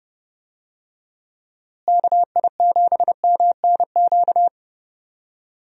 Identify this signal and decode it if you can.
Morse code, heard as a keyed tone, about 30 wpm, 700 Hz. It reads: KI7MNQ
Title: USA Call Signs - 6 Characters - Repeated 2x - 30wpm